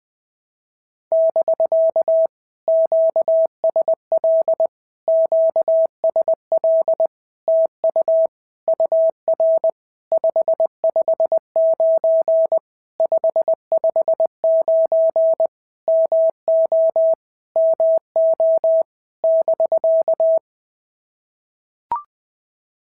Morse code, heard as a keyed tone, about 20 wpm, 650 Hz. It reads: <BK> QSL QSL TU UR 559 559 MO MO <BK>
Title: POTA QSO Practice - 20wpm